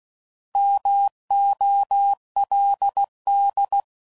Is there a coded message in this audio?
Yes: Morse code from 0.6 to 3.8 s, heard as a keyed tone, about 16 wpm, 800 Hz. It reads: MOLD